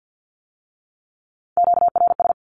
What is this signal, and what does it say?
Morse code, heard as a keyed tone, about 50 wpm, 700 Hz. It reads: QRS
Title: Q-Codes with Spoken Meaning - 50wpm